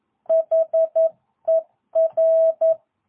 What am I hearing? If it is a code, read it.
Morse code, heard as a keyed tone, about 11 wpm, 650 Hz. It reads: HER
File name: jh8cbh / 2022cw